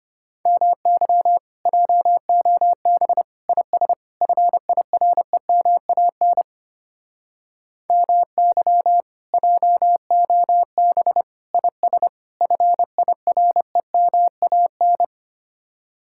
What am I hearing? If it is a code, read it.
Morse code, heard as a keyed tone, about 31 wpm, 700 Hz. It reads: MY JOB IS FIREMAN MY JOB IS FIREMAN
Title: Speed-Racing Common QSO Phrases - 25wpm